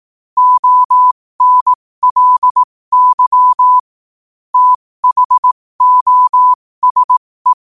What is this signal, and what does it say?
Morse code, heard as a keyed tone, about 18 wpm, spaced as for 15 wpm, 1000 Hz. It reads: ONLY THOSE
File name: qotdfiles / 15wpm